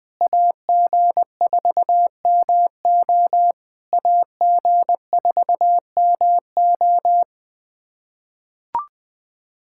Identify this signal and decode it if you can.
Morse code, heard as a keyed tone, about 20 wpm, 700 Hz. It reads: AG4MO AG4MO
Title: SOTA QSO Practice 20wpm